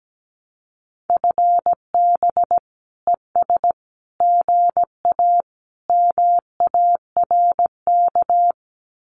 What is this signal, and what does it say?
Morse code, heard as a keyed tone, about 17 wpm, 700 Hz. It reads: FB ES GA MARK
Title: Common QSO Phrases Encore 17wpm